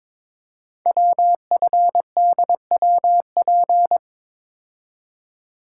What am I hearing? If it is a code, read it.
Morse code, heard as a keyed tone, about 22 wpm, 700 Hz. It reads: WFDWP
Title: Code Groups - 5 Letters - 22wpm